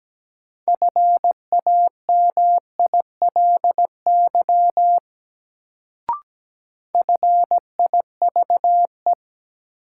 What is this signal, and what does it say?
Morse code, heard as a keyed tone, about 17 wpm, 700 Hz. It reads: FAMILY FIVE
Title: Sentences from Top 400 Words - New Word Drill - 17wpm